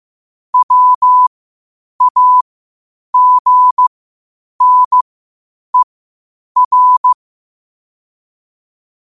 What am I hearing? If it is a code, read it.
Morse code, heard as a keyed tone, about 15 wpm, spaced as for 8 wpm, 1000 Hz. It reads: WAGNER
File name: qotdfiles / 8wpm